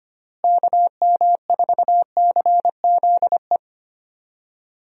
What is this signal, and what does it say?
Morse code, heard as a keyed tone, about 25 wpm, 700 Hz. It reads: KM4CZE
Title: USA Call Signs - 6 Characters - Repeated 3x - 25wpm